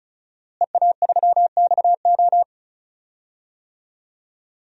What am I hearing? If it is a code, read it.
Morse code, heard as a keyed tone, about 34 wpm, 700 Hz. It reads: EA3XO
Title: DXCC-100 Call Signs - 5 Characters - Repeated 3x - 35wpm